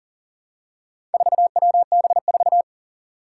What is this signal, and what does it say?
Morse code, heard as a keyed tone, about 40 wpm, 700 Hz. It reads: 4WB4
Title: Four Challenging Letter-Number Combinations 40wpm